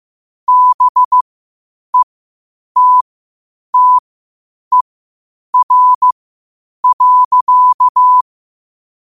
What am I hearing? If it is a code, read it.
Morse code, heard as a keyed tone, about 15 wpm, spaced as for 8 wpm, 1000 Hz. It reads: BETTER.
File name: qotdfiles / 8wpm